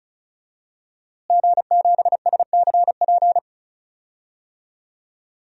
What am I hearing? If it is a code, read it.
Morse code, heard as a keyed tone, about 36 wpm, 700 Hz. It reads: G7SCP
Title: DXCC-100 Call Signs - 5 Characters - Repeated 2x - 35wpm